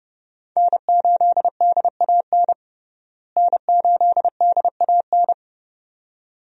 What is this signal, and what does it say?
Morse code, heard as a keyed tone, about 29 wpm, 700 Hz. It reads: N8DAN N8DAN
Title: USA Call Signs - 5 Characters - Repeated 2x - 30wpm